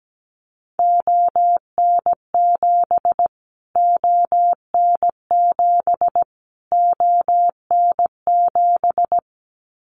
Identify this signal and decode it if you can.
Morse code, heard as a keyed tone, about 17 wpm, 700 Hz. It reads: ON7 ON7 ON7
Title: Callsign Prefixes for Top 25 DXCC - Repeated 3x - 17wpm